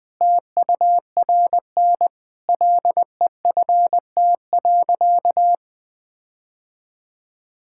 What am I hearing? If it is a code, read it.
Morse code, heard as a keyed tone, about 21 wpm, 700 Hz. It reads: TURN LEFT.
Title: Sentences with 5-Word Phrases - 20wpm